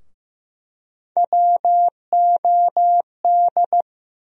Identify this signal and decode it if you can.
Morse code, heard as a keyed tone, about 15 wpm, 700 Hz. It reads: WOD
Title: USA Call Sign Postfixes - 3 Letters 15wpm